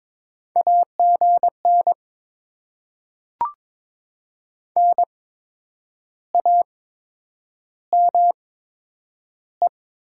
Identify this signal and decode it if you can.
Morse code, heard as a keyed tone, about 23 wpm, spaced as for 9 wpm, 700 Hz. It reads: AGN NAME
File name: Top 20 QSO Elements - 8x Character Spacing - 22wpm